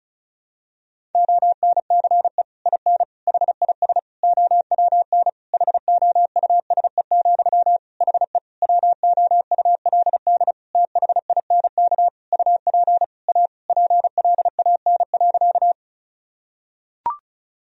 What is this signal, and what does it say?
Morse code, heard as a keyed tone, about 36 wpm, 700 Hz. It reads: ONCE IN HIS OWN HOUSE, HE WOULD THINK UP A PLAN.
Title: Sentences from Top 1000 Words - Review Part 2 - 35wpm